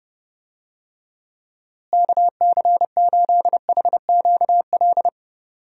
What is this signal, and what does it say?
Morse code, heard as a keyed tone, about 32 wpm, 700 Hz. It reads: KC8HQL
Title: USA Call Signs - 6 Characters - Repeated 2x - 30wpm